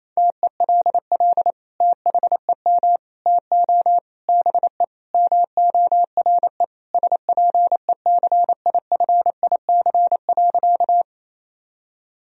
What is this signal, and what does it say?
Morse code, heard as a keyed tone, about 28 wpm, 700 Hz. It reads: TELL THEM TO BE MORE SPECIFIC.
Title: Sentences from Top 1000 Words - Review Part 3 - 28wpm